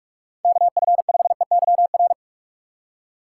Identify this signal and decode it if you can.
Morse code, heard as a keyed tone, about 44 wpm, 700 Hz. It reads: KF5EYR